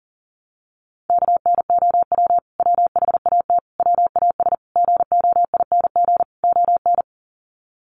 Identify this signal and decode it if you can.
Morse code, heard as a keyed tone, about 37 wpm, 700 Hz. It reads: KNOW WHAT WAS GOING ON
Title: Sets of 5 Words Encore2 40wpm